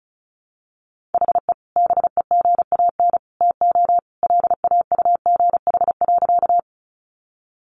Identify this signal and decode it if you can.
Morse code, heard as a keyed tone, about 34 wpm, 700 Hz. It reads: HE BEGAN TO LAUGH.
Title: Sentences from Top 500 Words 35wpm